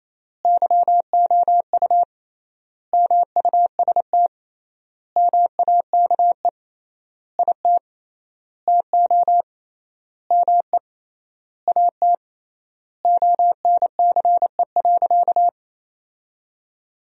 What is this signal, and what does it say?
Morse code, heard as a keyed tone, about 28 wpm, 700 Hz. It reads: YOU MUST MAKE IT TO ME AT ONCE.
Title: 3x Word-Spacing - Sentences from Top 500 Words - 28wpm